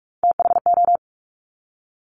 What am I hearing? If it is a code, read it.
Morse code, heard as a keyed tone, about 46 wpm, 700 Hz. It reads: THO
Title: Top Three Letter Combinations in Words 45wpm